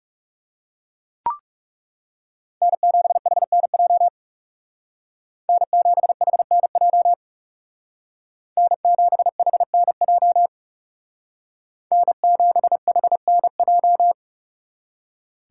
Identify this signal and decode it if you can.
Morse code, heard as a keyed tone, about 36 wpm, 700 Hz. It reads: N7HNJ N7HNJ N7HNJ N7HNJ
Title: Speed Racing - USA 1x3 Call Signs 30wpm